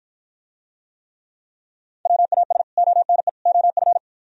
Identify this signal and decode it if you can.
Morse code, heard as a keyed tone, about 53 wpm, 700 Hz. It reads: WAS ONE OF